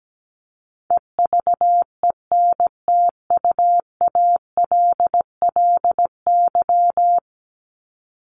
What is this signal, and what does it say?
Morse code, heard as a keyed tone, about 17 wpm, 700 Hz. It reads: EVENTUALLY
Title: LY Suffix - Top 100 Words with Base Words 17wpm